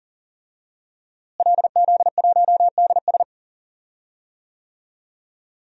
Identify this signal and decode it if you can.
Morse code, heard as a keyed tone, about 37 wpm, 700 Hz. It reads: LZ1DS
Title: DXCC-100 Call Signs - 5 Characters - Repeated 2x - 40wpm